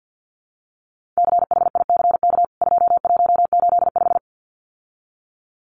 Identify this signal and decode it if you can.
Morse code, heard as a keyed tone, about 51 wpm, 700 Hz. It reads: CHICK 2185
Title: CWT Exchange 50wpm